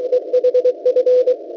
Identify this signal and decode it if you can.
Morse code, heard as a keyed tone, about 23 wpm, 500 Hz, with radio noise. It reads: EHF
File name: WebSDR records / w7pua-2.ddns.net 17016.80 cw